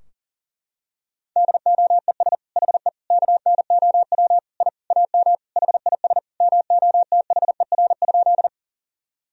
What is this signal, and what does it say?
Morse code, heard as a keyed tone, about 40 wpm, 700 Hz. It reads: DOES HE KNOW I AM HIS MOTHER?
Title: Sentences from Top 1000 Words - Review Part 2 - 40wpm